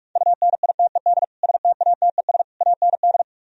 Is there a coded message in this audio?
Yes: Morse code from 0.2 to 3.2 s, heard as a keyed tone, about 49 wpm, 700 Hz. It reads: UNITED STATES AND